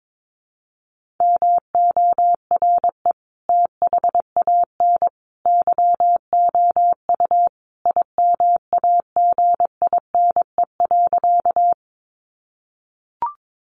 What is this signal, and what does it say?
Morse code, heard as a keyed tone, about 22 wpm, 700 Hz. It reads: MORE THAN YOU IMAGINE.
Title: Sentences from Top 1400 Words 22wpm